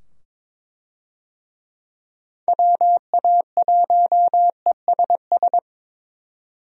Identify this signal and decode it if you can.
Morse code, heard as a keyed tone, about 22 wpm, 700 Hz. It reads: WA1ESS
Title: USA Call Signs - 6 Characters - 22wpm